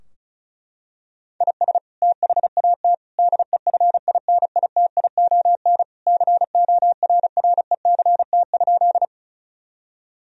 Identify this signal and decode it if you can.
Morse code, heard as a keyed tone, about 34 wpm, 700 Hz. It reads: IS THAT DEFINITION CORRECT?
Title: Sentences from Top 1000 Words - Review Part 4 - 35wpm